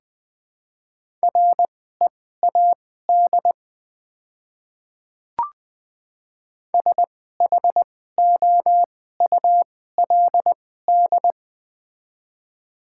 Morse code, heard as a keyed tone, about 20 wpm, spaced as for 15 wpm, 700 Hz. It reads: READ SHOULD
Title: 2x Character Spacing - Top 300 Words - 20wpm